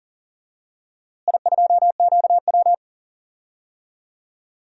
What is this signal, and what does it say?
Morse code, heard as a keyed tone, about 40 wpm, 700 Hz. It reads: I2QW
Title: DXCC-100 Call Signs - 4 Characters - Repeated 2x - 40wpm